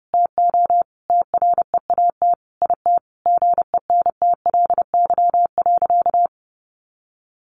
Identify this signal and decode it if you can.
Morse code, heard as a keyed tone, about 30 wpm, 700 Hz. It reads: TO TREAT IT GENTLY.
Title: Sentences from Top 1500 Words 30wpm